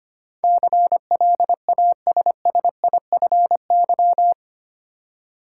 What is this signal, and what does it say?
Morse code, heard as a keyed tone, about 25 wpm, 700 Hz. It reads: CLASSIFY
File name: 8 Letter Words - First 3 Letters Given - 25wpm